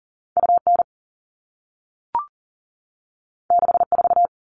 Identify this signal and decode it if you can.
Morse code, heard as a keyed tone, about 41 wpm, 700 Hz. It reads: UN 64
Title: All Cut Numbers - 2-Digit 40wpm